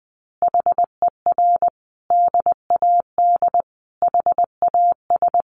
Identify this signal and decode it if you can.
Morse code, heard as a keyed tone, about 20 wpm, 700 Hz. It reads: HER DAD HAS
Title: Additional Easy Phrases 20wpm